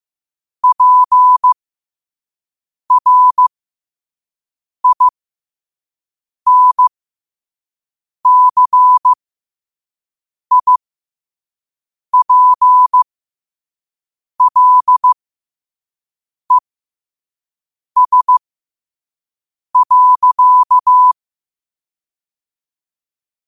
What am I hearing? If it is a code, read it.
Morse code, heard as a keyed tone, about 15 wpm, spaced as for 5 wpm, 1000 Hz. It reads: PRINCIPLES.
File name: qotdfiles / 5wpm